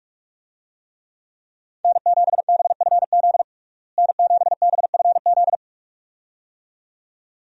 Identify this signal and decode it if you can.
Morse code, heard as a keyed tone, about 46 wpm, 700 Hz. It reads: N7BFZ N7BFZ
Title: USA Call Signs - 5 Characters - Repeated 2x - 45wpm